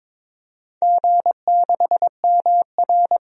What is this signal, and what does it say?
Morse code, heard as a keyed tone, about 23 wpm, 700 Hz. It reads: G6MR
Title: DXCC-100 Call Signs - 4 Characters - 22wpm